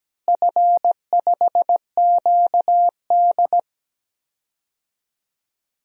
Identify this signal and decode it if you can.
Morse code, heard as a keyed tone, about 17 wpm, 700 Hz. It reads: F5QD
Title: DXCC-100 Call Signs - 4 Characters - Repeated 3x - 17wpm